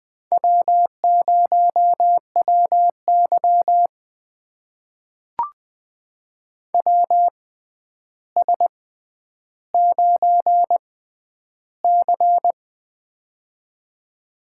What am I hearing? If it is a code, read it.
Morse code, heard as a keyed tone, about 20 wpm, spaced as for 10 wpm, 700 Hz. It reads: W0WY WS9C
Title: USA Call Signs - 4 Characters - 6x Character Spacing - 20wpm